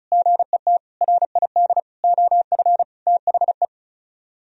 Morse code, heard as a keyed tone, about 46 wpm, spaced as for 41 wpm, 700 Hz. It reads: GET RID OF THE
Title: Speed-Racing - Sets of 5 Words 35wpm